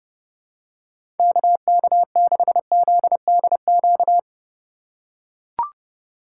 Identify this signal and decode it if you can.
Morse code, heard as a keyed tone, about 30 wpm, 700 Hz. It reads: KK6ZDQ
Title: USA Call Signs - 6 Characters - 2x Character Spacing - 30wpm